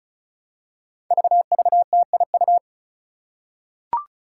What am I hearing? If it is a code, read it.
Morse code, heard as a keyed tone, about 36 wpm, 700 Hz. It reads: VVTIU
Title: Five Challenging Letter-Number Combinations 35wpm